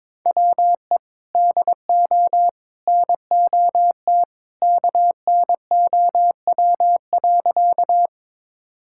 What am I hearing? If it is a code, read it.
Morse code, heard as a keyed tone, about 22 wpm, 700 Hz. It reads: WE DO NOT KNOW.
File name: practice / 22wpm / Sentences with 3-Word Phrases - 22wpm